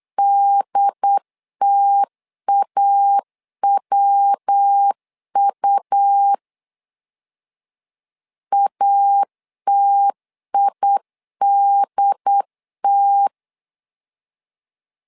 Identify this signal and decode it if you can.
Morse code, heard as a keyed tone, about 8 wpm, 800 Hz. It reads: DTAWU ATIDT
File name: LEZIONI / 4 Q 8wpm 2732